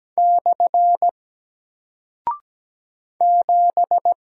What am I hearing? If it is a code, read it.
Morse code, heard as a keyed tone, about 17 wpm, 700 Hz. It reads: / 7
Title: Single Character (Letter:Number:Punctuation:Prosign) Alt 17wpm